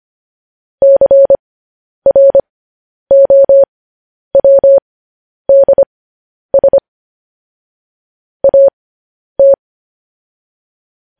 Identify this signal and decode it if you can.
Morse code, heard as a keyed tone, about 25 wpm, spaced as for 10 wpm, 550 Hz. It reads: CROWDS AT